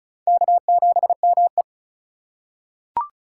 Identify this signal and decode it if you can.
Morse code, heard as a keyed tone, about 36 wpm, 700 Hz. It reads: K7ME
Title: USA Call Signs - 3x Character Spacing - 35wpm